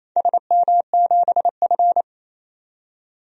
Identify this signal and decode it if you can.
Morse code, heard as a keyed tone, about 27 wpm, 700 Hz. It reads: SM7F
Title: DXCC-100 Call Signs - 4 Characters - 28wpm